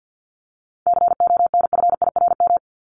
Morse code, heard as a keyed tone, about 50 wpm, 700 Hz. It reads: CONFIRM
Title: Top 1751-2000 Words 50wpm